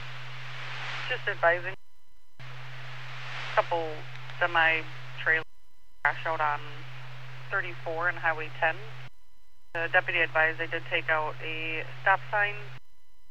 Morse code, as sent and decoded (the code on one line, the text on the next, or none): none